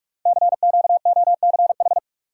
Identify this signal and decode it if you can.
Morse code, heard as a keyed tone, about 44 wpm, 700 Hz. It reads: CQOCH
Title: Code Groups - 5 Letter:Numbers - 45wpm